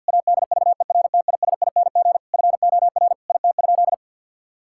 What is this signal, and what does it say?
Morse code, heard as a keyed tone, about 52 wpm, 700 Hz. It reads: ADVERTISING FOR IT?
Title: Sentences from Top 1000 Words - Review Part 6 - 50wpm